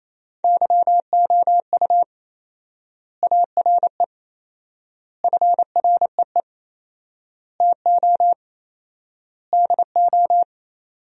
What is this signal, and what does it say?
Morse code, heard as a keyed tone, about 28 wpm, 700 Hz. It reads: YOU ARE FREE TO DO
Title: 4x Word-Spacing - Sentences from Top 500 Words - 28wpm